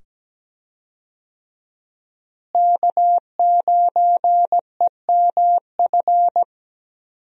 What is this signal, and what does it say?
Morse code, heard as a keyed tone, about 17 wpm, 700 Hz. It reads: K9EMF
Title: USA Call Signs - 5 Characters - Repeated 2x - 17wpm